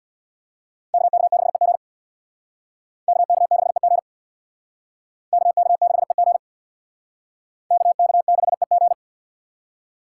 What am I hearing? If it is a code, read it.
Morse code, heard as a keyed tone, about 61 wpm, 700 Hz. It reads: KK6EG KK6EG KK6EG KK6EG
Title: Speed Racing - USA 2x2 Call Signs 50wpm